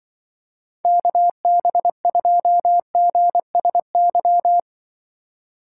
Morse code, heard as a keyed tone, about 23 wpm, 700 Hz. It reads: KB2GSY
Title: Speed Racing - USA 2x3 Call Signs 20wpm